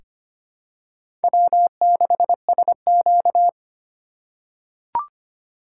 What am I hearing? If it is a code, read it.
Morse code, heard as a keyed tone, about 25 wpm, 700 Hz. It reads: W6SQ
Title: USA Call Signs - 4 Characters - 3x Character Spacing - 25wpm